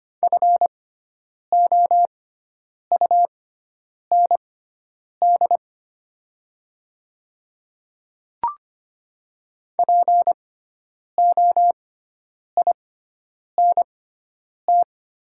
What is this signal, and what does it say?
Morse code, heard as a keyed tone, about 25 wpm, spaced as for 9 wpm, 700 Hz. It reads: FOUND POINT
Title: 6x Character Spacing - Top 300 Words - 25wpm